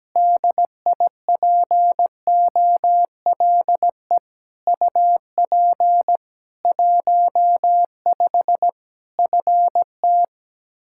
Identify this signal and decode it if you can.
Morse code, heard as a keyed tone, about 17 wpm, 700 Hz. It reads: DIPOLE UP 15 FT
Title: Practice QSO - Exchange 2 - Info - 17wpm